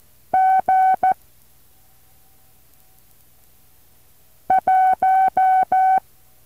Morse code, heard as a keyed tone, about 14 wpm, 760 Hz. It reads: G 1
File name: intl morse code2